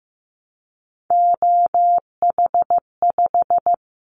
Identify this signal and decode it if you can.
Morse code, heard as a keyed tone, about 15 wpm, 700 Hz. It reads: OH5
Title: Callsign Prefixes for Top 25 DXCC - 15wpm